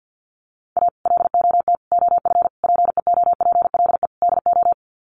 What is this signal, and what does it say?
Morse code, heard as a keyed tone, about 50 wpm, 700 Hz. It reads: A LOT OF PEOPLE DO
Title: Sets of 5 Words 50wpm